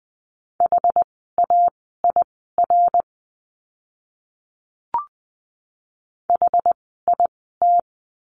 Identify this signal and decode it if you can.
Morse code, heard as a keyed tone, about 20 wpm, spaced as for 14 wpm, 700 Hz. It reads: HAIR HIT
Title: 101 Lesson 17f - Letter u - Words 20wpm